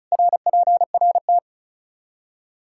Morse code, heard as a keyed tone, about 36 wpm, 700 Hz. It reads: RPRT
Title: Practice QSO - Exchange 2 - Info - 35wpm